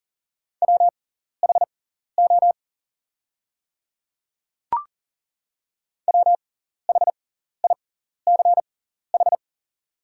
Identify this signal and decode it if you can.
Morse code, heard as a keyed tone, about 40 wpm, spaced as for 14 wpm, 690 Hz. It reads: WHO WHICH
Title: 6x Character Spacing - Top 100 Words - 40wpm